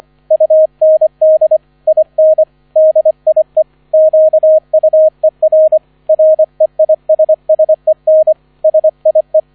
Morse code, heard as a keyed tone, about 25 wpm, 630 Hz, with radio noise. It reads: UND IN DIE QUER REISSEN SIE